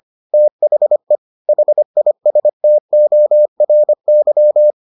Morse code, heard as a keyed tone, about 25 wpm, 600 Hz. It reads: THE HISTORY